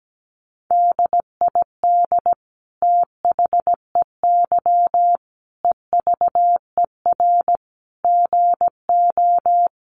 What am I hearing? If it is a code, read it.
Morse code, heard as a keyed tone, about 17 wpm, 700 Hz. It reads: DID THEY EVER GO
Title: Sentences from Top 400 Words 17wpm